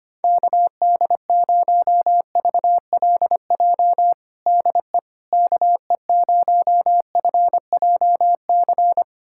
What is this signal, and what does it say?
Morse code, heard as a keyed tone, about 25 wpm, 700 Hz. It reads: KD0VLJ DE KE0FJC
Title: USA Callsign DE Callsign - 6 Characters - 25wpm